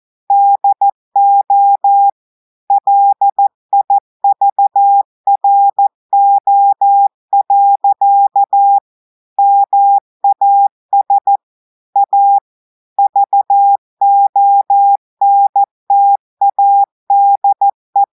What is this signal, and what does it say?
Morse code, heard as a keyed tone, about 14 wpm, 800 Hz. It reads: DO LIVRO. MAS A VONTADE
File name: audios / portugais / portugais texte en clair 1 vitesse 14 mots minutes